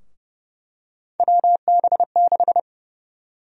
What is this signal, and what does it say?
Morse code, heard as a keyed tone, about 30 wpm, 700 Hz. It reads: WB6
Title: USA Call Sign Prefixes - Repeated 2x - 30wpm